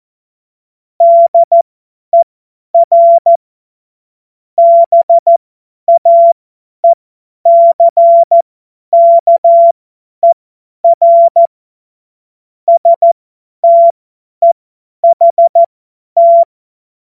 Morse code, heard as a keyed tone, about 14 wpm, spaced as for 10 wpm, 680 Hz. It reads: DER BAECKER STEHT
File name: Morsekurs / Witze / Witz 18. Mai 2021, 50 F70